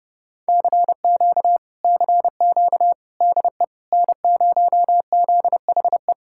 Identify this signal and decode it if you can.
Morse code, heard as a keyed tone, about 30 wpm, 700 Hz. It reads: CQ CQ DE N0ZHE